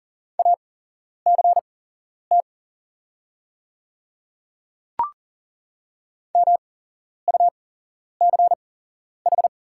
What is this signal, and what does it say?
Morse code, heard as a keyed tone, about 40 wpm, spaced as for 11 wpm, 690 Hz. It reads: ACT MUCH
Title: 8x Character Spacing - Top 200 Words - 40wpm